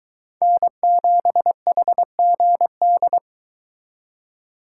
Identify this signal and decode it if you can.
Morse code, heard as a keyed tone, about 22 wpm, 700 Hz. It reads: N7HGD